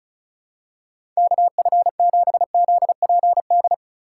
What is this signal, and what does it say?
Morse code, heard as a keyed tone, about 36 wpm, 700 Hz. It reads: KF7ZPD
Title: USA Call Signs - 3x Character Spacing - 35wpm